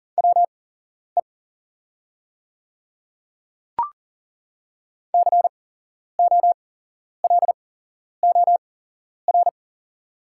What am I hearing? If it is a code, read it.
Morse code, heard as a keyed tone, about 40 wpm, spaced as for 11 wpm, 690 Hz. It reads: WE COLOR
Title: 8x Character Spacing - Top 300 Words - 40wpm